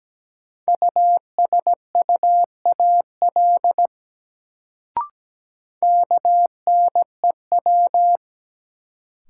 Morse code, heard as a keyed tone, about 17 wpm, 700 Hz. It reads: USUAL KNEW
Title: Sentences from Top 400 Words - New Word Drill - 17wpm